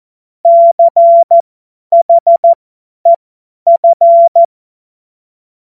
Morse code, heard as a keyed tone, about 14 wpm, spaced as for 10 wpm, 680 Hz. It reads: CHEF
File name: Morsekurs / Witze / Witz 20. Dezember 2022 50 F70